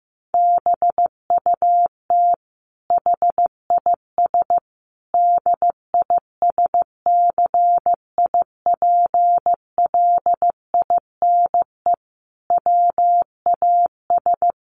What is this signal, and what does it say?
Morse code, heard as a keyed tone, about 15 wpm, 700 Hz. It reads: BUT HIS DISCIPLINE WAS
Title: Sentences from Top 1400 Words 15wpm